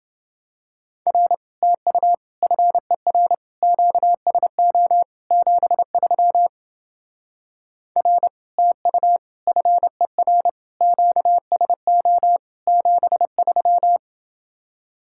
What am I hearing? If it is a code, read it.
Morse code, heard as a keyed tone, about 29 wpm, 700 Hz. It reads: R TU FER QSO 73 R TU FER QSO 73